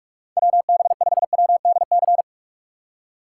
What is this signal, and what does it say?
Morse code, heard as a keyed tone, about 44 wpm, 690 Hz. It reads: WB5WDC
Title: USA Call Signs - 6 Characters - 3x Character Spacing - 45wpm